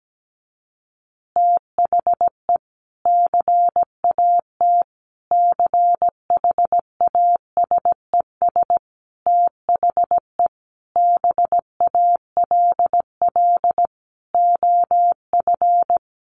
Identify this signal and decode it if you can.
Morse code, heard as a keyed tone, about 17 wpm, 700 Hz. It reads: THE CAT CHASES THE BALL OF
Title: Additional Easy Phrases 17wpm